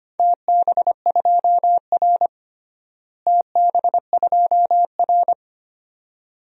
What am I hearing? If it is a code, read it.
Morse code, heard as a keyed tone, about 25 wpm, 700 Hz. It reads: TB2R TB2R
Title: DXCC-100 Call Signs - 4 Characters - Repeated 2x - 25wpm